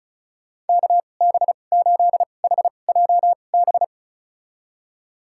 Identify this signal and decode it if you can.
Morse code, heard as a keyed tone, about 36 wpm, spaced as for 25 wpm, 700 Hz. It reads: KB8HJB
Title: USA Call Signs - 6 Characters - 2x Character Spacing - 35wpm